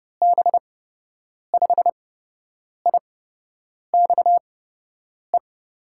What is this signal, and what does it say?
Morse code, heard as a keyed tone, about 30 wpm, spaced as for 8 wpm, 700 Hz. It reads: B5IXE
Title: USA Call Signs - 6 Characters - 8x Character Spacing - 30wpm